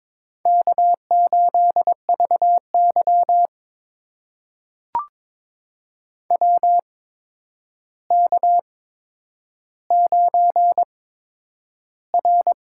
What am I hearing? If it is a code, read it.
Morse code, heard as a keyed tone, about 22 wpm, spaced as for 9 wpm, 700 Hz. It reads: K8VY WK9R
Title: USA Call Signs - 4 Characters - 8x Character Spacing - 22wpm